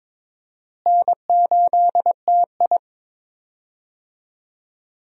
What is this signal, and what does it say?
Morse code, heard as a keyed tone, about 23 wpm, 700 Hz. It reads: N8TI
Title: USA Call Signs - 4 Characters - Repeated 2x - 22wpm